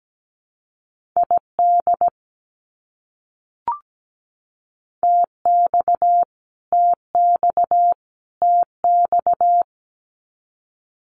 Morse code, heard as a keyed tone, about 17 wpm, 700 Hz. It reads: ID TX TX TX
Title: US States Repeated 3x 17wpm